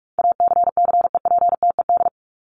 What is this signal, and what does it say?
Morse code, heard as a keyed tone, about 44 wpm, 700 Hz. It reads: ACCEPTED
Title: ED Suffix - Top 100 Words with Base Words 45wpm